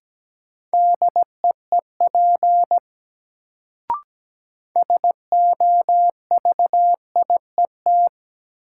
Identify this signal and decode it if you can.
Morse code, heard as a keyed tone, about 17 wpm, 700 Hz. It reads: DEEP SOVIET
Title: Sets of 5 Words Encore - New Word Drill - 17wpm